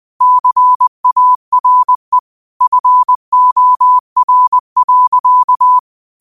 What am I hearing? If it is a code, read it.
Morse code, heard as a keyed tone, about 20 wpm, 1000 Hz. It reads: CARE FOR.
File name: qotdfiles / 20wpm